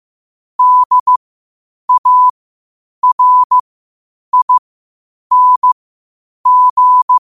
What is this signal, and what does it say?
Morse code, heard as a keyed tone, about 15 wpm, spaced as for 8 wpm, 1000 Hz. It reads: DARING
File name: qotdfiles / 8wpm